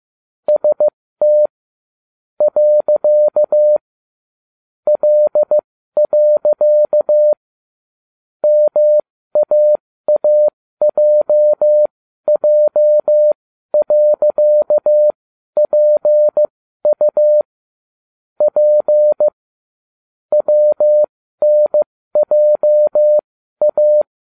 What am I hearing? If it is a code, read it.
Morse code, heard as a keyed tone, about 15 wpm, spaced as for 12 wpm, 600 Hz. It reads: ST . L. MAAJJ.PU P WNJA